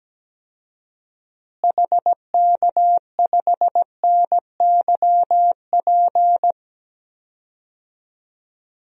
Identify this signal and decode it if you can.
Morse code, heard as a keyed tone, about 17 wpm, 700 Hz. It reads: HK5NYP
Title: DXCC-100 Call Signs - 6 Characters - Repeated 2x - 17wpm